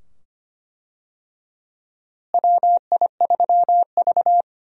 Morse code, heard as a keyed tone, about 25 wpm, 700 Hz. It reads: WI3V
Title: USA Call Signs - 4 Characters - Repeated 2x - 25wpm